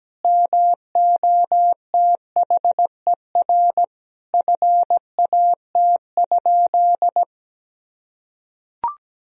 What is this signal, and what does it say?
Morse code, heard as a keyed tone, about 17 wpm, 700 Hz. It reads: MOTHER FAT?
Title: Sentences from Top 1000 Words - Review Part 4 - 17wpm